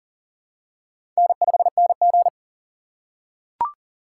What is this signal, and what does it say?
Morse code, heard as a keyed tone, about 38 wpm, 700 Hz. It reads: N5NG
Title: USA Call Signs - 4 Characters - 2x Character Spacing - 40wpm